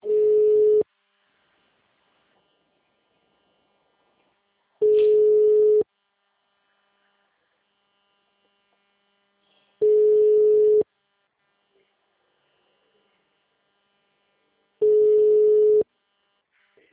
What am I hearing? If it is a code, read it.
Ringback tone (Europe, 425 Hz).